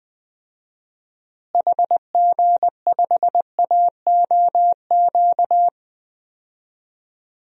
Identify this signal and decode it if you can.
Morse code, heard as a keyed tone, about 20 wpm, 700 Hz. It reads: HG5AOQ
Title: DXCC-100 Call Signs - 6 Characters - Repeated 3x - 20wpm